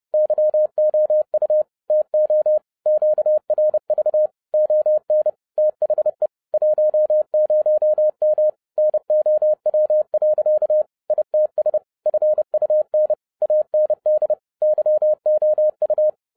Morse code, heard as a keyed tone, about 30 wpm, 610 Hz. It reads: YOU TO QRV ON THE 10M NOW. ITS FUN AND YOU
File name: 2012-01-mp3 / n12-I DIDNT MISS THE SOLAR CYCLE 24-30wpm